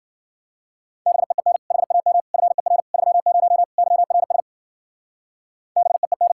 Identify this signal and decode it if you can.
Morse code, heard as a keyed tone, about 48 wpm, 700 Hz. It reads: BEEN HAM FER 30 YRS BEEN
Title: Speed-Racing Common QSO Phrases - 40wpm